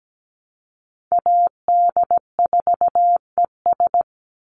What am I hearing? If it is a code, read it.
Morse code, heard as a keyed tone, about 17 wpm, 700 Hz. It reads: AD4ES
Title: Call Signs Repeated 3x - 17wpm